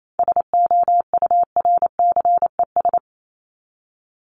Morse code, heard as a keyed tone, about 28 wpm, 700 Hz. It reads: SOURCES
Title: ES:S Suffix - Top 100 Words with Base Words 28wpm